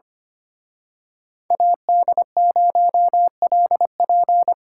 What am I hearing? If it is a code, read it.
Morse code, heard as a keyed tone, about 30 wpm, 700 Hz. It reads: AD0LP
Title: Speed Racing - USA 2x2 Call Signs 25wpm